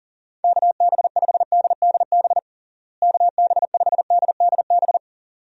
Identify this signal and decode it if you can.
Morse code, heard as a keyed tone, about 40 wpm, 700 Hz. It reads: KB5DDB KB5DDB
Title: USA Call Signs - 6 Characters - Repeated 2x - 40wpm